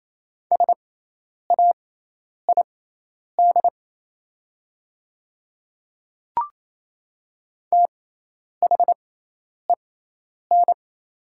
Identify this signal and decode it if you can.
Morse code, heard as a keyed tone, about 27 wpm, spaced as for 10 wpm, 700 Hz. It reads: SAID THEN
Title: 6x Character Spacing - Top 100 Words - 28wpm